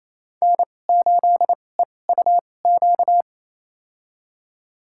Morse code, heard as a keyed tone, about 28 wpm, spaced as for 20 wpm, 700 Hz. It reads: N8EUQ
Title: USA Call Signs - 5 Characters - 2x Character Spacing - 28wpm